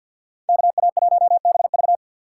Morse code, heard as a keyed tone, about 52 wpm, 700 Hz. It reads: KA1BV
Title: USA Call Signs - 5 Characters - 3x Character Spacing - 50wpm